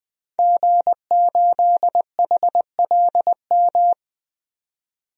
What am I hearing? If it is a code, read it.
Morse code, heard as a keyed tone, about 21 wpm, 700 Hz. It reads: G8HLM
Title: DXCC-100 Call Signs - 5 Characters - 20wpm